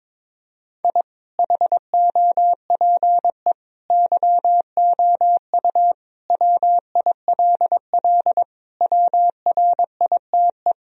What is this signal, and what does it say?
Morse code, heard as a keyed tone, about 22 wpm, 700 Hz. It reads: I HOPE YOU WILL WRITE
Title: Sentences from Top 1000 Words - Review Part 1 - 22wpm